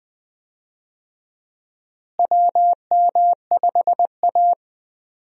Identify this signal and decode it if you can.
Morse code, heard as a keyed tone, about 20 wpm, 700 Hz. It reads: WM5A
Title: USA Call Signs - 4 Characters - Repeated 3x - 20wpm